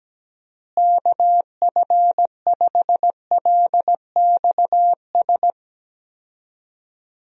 Morse code, heard as a keyed tone, about 17 wpm, 700 Hz. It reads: KF5LXS
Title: USA Call Signs - 6 Characters - Repeated 2x - 17wpm